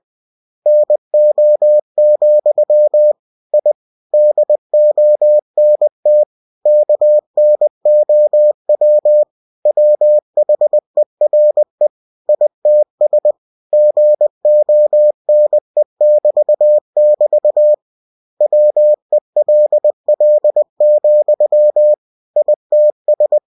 Morse code, heard as a keyed tone, about 20 wpm, 600 Hz. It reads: NO, I DONT KNOW WHERE ITS GONE<BT><BT> WELL, ITS
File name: cw-books / winnie-the-pooh / winnie-20wpm-page-0039